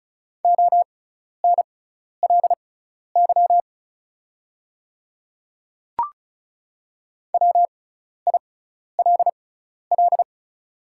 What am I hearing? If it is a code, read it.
Morse code, heard as a keyed tone, about 36 wpm, spaced as for 12 wpm, 700 Hz. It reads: ONLY WILL
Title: 6x Character Spacing - Top 200 Words - 35wpm